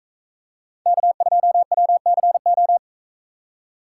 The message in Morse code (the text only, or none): K2WCO